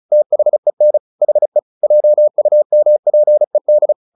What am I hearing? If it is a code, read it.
Morse code, heard as a keyed tone, about 35 wpm, 600 Hz. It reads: THEN HE JUMPED
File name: the old man and the sea 2 35wpm